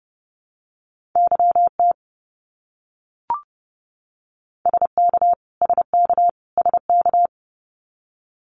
Morse code, heard as a keyed tone, about 30 wpm, 700 Hz. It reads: YT SK SK SK
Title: Canadian Provinces and Territories - Repeated 3x - 30wpm